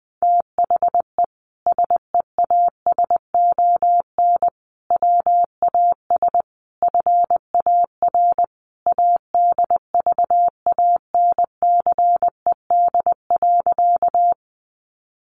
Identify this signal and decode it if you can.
Morse code, heard as a keyed tone, about 20 wpm, 700 Hz. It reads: THE SEASON WAS FAR ADVANCED.
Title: Sentences from Top 1400 Words 20wpm